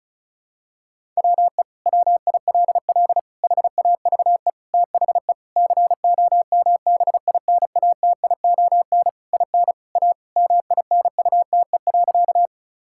Morse code, heard as a keyed tone, about 34 wpm, 700 Hz. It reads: WE WILL HAVE THE COMBINATION IN A MINUTE.
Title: Sentences from Top 1000 Words - Review Part 6 - 35wpm